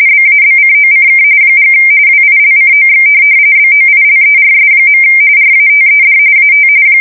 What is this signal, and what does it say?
RTTY, about 45.45 baud. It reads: BABCOCK & WILCOX BOILERS. RATED AT
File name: fsk170 / USS-Massachusetts